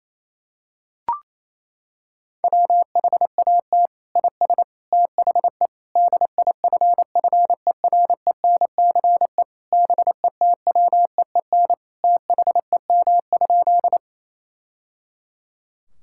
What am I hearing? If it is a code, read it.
Morse code, heard as a keyed tone, about 27 wpm, 700 Hz. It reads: WHAT IS THE DIFFERENCE BETWEEN THEM?
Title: Sentences from Top 1000 Words - Review Part 5 - 28wpm